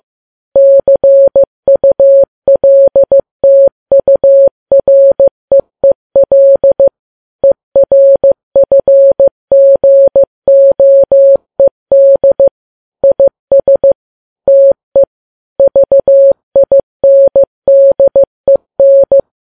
Morse code, heard as a keyed tone, about 15 wpm, 550 Hz. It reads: CULTUREEL ERFGOED IS TE VINDEN
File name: Files Web tls / Morse on9fv / morse15kps